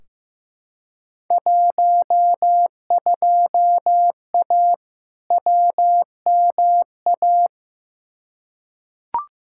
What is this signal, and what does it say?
Morse code, heard as a keyed tone, about 15 wpm, 700 Hz. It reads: 12A WMA
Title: Field Day Exchanges Encore - 15wpm